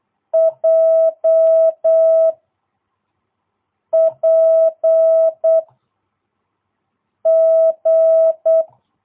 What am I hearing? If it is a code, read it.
Morse code, heard as a keyed tone, about 8 wpm, 650 Hz. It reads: J P G
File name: jh8cbh / 2022cw